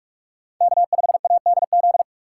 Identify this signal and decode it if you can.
Morse code, heard as a keyed tone, about 47 wpm, 700 Hz. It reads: K5ADZ
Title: USA Call Signs - 5 Characters - 2x Character Spacing - 45wpm